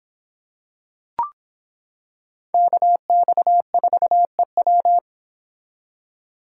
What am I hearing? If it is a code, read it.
Morse code, heard as a keyed tone, about 26 wpm, 700 Hz. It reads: KX4EW